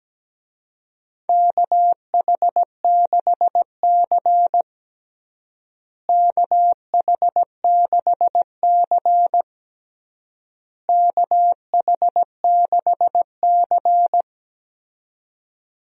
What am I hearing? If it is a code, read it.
Morse code, heard as a keyed tone, about 17 wpm, 700 Hz. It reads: KH6C KH6C KH6C
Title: DXCC-100 Call Signs - 4 Characters - Repeated 3x - 17wpm